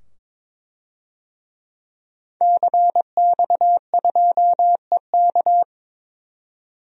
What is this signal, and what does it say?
Morse code, heard as a keyed tone, about 23 wpm, 700 Hz. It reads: CX2EK
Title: DXCC-100 Call Signs - 5 Characters - 22wpm